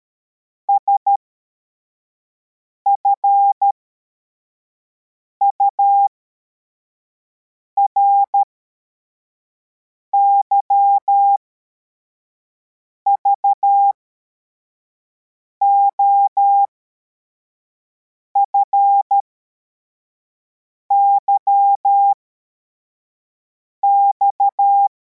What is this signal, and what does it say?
Morse code, heard as a keyed tone, about 13 wpm, spaced as for 4 wpm, 800 Hz. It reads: SFURY VOFYX
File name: formation / cw / Text71